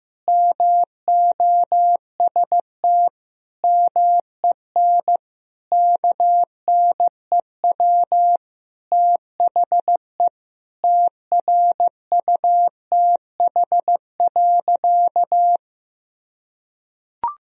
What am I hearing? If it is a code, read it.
Morse code, heard as a keyed tone, about 15 wpm, 700 Hz. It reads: MOST MEN KNEW THE TRUTH.
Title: Sentences from Top 1000 Words - Review Part 1 - 15wpm